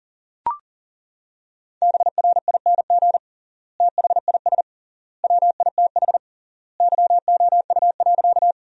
Morse code, heard as a keyed tone, about 37 wpm, 700 Hz. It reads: BRING THIS WITH YOU.
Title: 3x Word-Spacing - Sentences from Top 500 Words - 40wpm